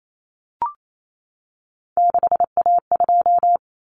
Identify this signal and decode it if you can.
Morse code, heard as a keyed tone, about 28 wpm, 700 Hz. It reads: 6A2
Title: Common Cut Numbers - 3-Digit 28wpm